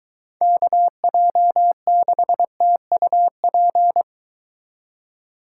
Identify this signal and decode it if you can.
Morse code, heard as a keyed tone, about 23 wpm, 700 Hz. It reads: KJ6TUP